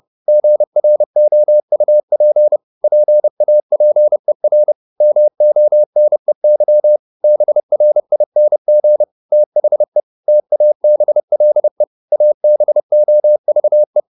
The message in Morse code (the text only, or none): GROUP PAPER MONEY BRING THE TABLE ABOVE